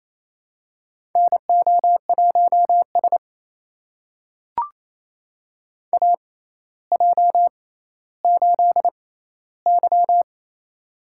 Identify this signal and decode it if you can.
Morse code, heard as a keyed tone, about 27 wpm, spaced as for 14 wpm, 700 Hz. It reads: NO1S AJ8Y
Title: USA Call Signs - 4 Characters - 6x Character Spacing - 28wpm